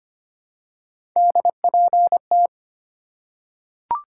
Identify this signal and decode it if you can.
Morse code, heard as a keyed tone, about 25 wpm, 700 Hz. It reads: DPT
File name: Three Letter-Number Combinations - 25wpm